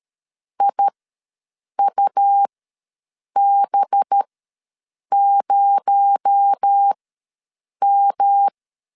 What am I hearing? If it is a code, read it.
Morse code, heard as a keyed tone, about 13 wpm, spaced as for 7 wpm, 800 Hz. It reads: IUB0M
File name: LEZIONI / 9 2867 ws 12wpm 3431